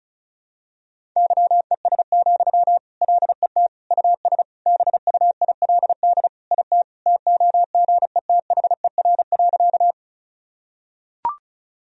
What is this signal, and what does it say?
Morse code, heard as a keyed tone, about 34 wpm, 700 Hz. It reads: YES, LET US BUILD IT TOGETHER.
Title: Sentences from Top 1000 Words - Review Part 1 - 35wpm